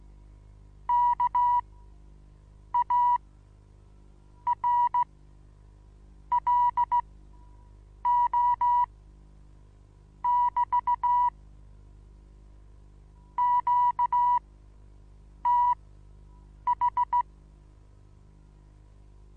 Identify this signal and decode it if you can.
Morse code, heard as a keyed tone, about 16 wpm, spaced as for 6 wpm, 970 Hz. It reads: KARLO<BT> QTH
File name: Onlinelehrgang / morsen / DJ4UF Morsen25